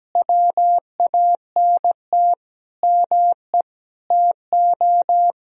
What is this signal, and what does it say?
Morse code, heard as a keyed tone, about 17 wpm, 700 Hz. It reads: WANT ME TO
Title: Sentences from Top 200 Words and Abbreviations 17wpm